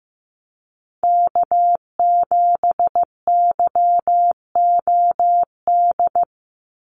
Morse code, heard as a keyed tone, about 15 wpm, 700 Hz. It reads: K7YOD
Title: USA Call Signs 15wpm